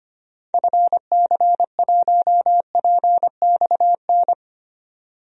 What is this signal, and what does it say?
Morse code, heard as a keyed tone, about 25 wpm, 690 Hz. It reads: FC1PXN
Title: DXCC-100 Call Signs - 6 Characters - 25wpm